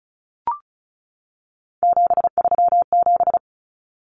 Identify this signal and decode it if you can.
Morse code, heard as a keyed tone, about 34 wpm, 700 Hz. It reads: 737
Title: Common Cut Numbers - 3-Digit 35wpm